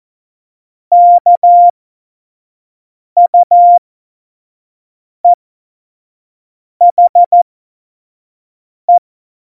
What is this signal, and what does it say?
Morse code, heard as a keyed tone, about 14 wpm, spaced as for 5 wpm, 700 Hz. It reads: KUEHE
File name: Morsekurs / Witze / Witz 16. Oktober 2018, 25 F70